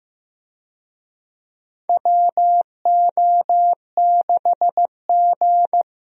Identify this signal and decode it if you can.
Morse code, heard as a keyed tone, about 15 wpm, 700 Hz. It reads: WO6G
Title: USA Call Signs - 4 Characters - Repeated 3x - 15wpm